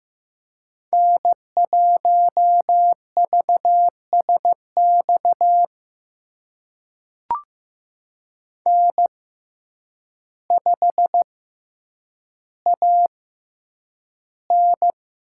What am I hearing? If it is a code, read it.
Morse code, heard as a keyed tone, about 15 wpm, spaced as for 8 wpm, 700 Hz. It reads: N1VSX N5AN
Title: USA Call Signs - 5 Characters - 6x Character Spacing - 15wpm